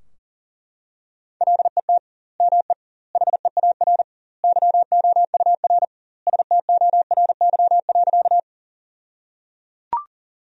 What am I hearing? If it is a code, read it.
Morse code, heard as a keyed tone, about 37 wpm, 700 Hz. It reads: LET ME HEAR YOUR STORY.
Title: 2x Word-Spacing - Sentences from Top 300 Words - 40wpm